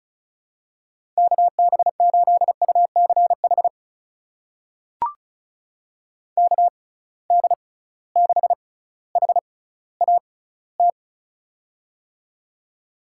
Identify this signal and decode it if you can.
Morse code, heard as a keyed tone, about 34 wpm, spaced as for 18 wpm, 700 Hz. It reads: KB8UCH KD6HAT
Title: USA Call Signs - 6 Characters - 6x Character Spacing - 35wpm